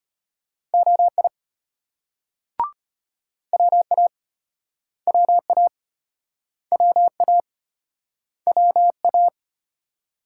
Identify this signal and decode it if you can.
Morse code, heard as a keyed tone, about 36 wpm, 700 Hz. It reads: OI WA WA WA WA